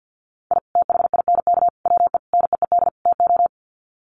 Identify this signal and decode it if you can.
Morse code, heard as a keyed tone, about 50 wpm, 700 Hz. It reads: I THINK WE NEED TO
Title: Sets of 5 Words 50wpm